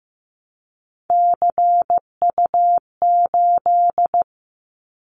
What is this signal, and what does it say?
Morse code, heard as a keyed tone, about 15 wpm, 700 Hz. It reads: CU8
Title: Callsign Prefixes for Top 50 DXCC - 15wpm